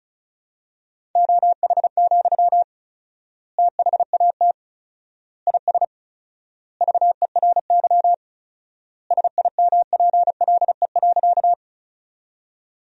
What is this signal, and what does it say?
Morse code, heard as a keyed tone, about 34 wpm, 700 Hz. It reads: OH, THAT IS VERY SIMPLE.
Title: 4x Word-Spacing - Sentences from Top 500 Words - 35wpm